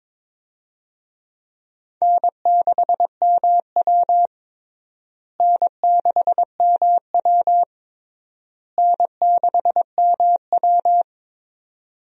Morse code, heard as a keyed tone, about 22 wpm, 700 Hz. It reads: N6MW N6MW N6MW
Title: USA Call Signs - 4 Characters - Repeated 3x - 22wpm